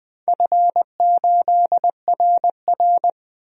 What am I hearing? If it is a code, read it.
Morse code, heard as a keyed tone, about 20 wpm, 700 Hz. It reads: F8RR
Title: DXCC-100 Call Signs - 4 Characters - 20wpm